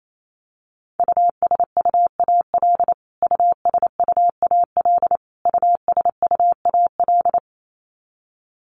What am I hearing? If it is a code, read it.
Morse code, heard as a keyed tone, about 28 wpm, 700 Hz. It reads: USUAL USUAL USUAL
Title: Top 500 Words Repeated 28wpm